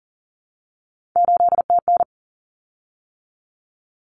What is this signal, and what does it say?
Morse code, heard as a keyed tone, about 40 wpm, 700 Hz. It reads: 8TN
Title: Common Cut Numbers - 3-Digit 40wpm